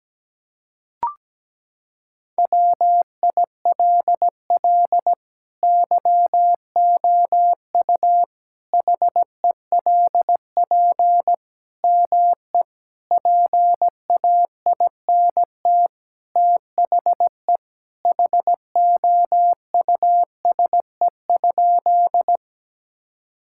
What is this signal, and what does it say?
Morse code, heard as a keyed tone, about 17 wpm, 700 Hz. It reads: WILL YOU HELP ME PAINT THE HOUSE?
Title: Sentences from Top 1000 Words - Review Part 6 - 17wpm